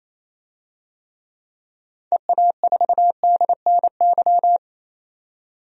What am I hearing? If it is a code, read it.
Morse code, heard as a keyed tone, about 28 wpm, 700 Hz. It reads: EA4DNY
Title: DXCC-100 Call Signs - 6 Characters - 28wpm